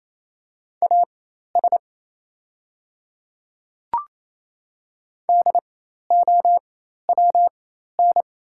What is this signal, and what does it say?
Morse code, heard as a keyed tone, about 27 wpm, spaced as for 13 wpm, 690 Hz. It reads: AS DOWN
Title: 4x Character Spacing - Top 100 Words - 28wpm